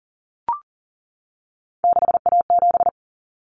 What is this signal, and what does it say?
Morse code, heard as a keyed tone, about 40 wpm, 700 Hz. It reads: 6A7
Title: Common Cut Numbers - 3-Digit 40wpm